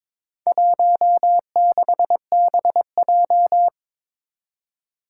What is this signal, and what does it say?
Morse code, heard as a keyed tone, about 22 wpm, 700 Hz. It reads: 16BJ
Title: Four Challenging Letter-Number Combinations 22wpm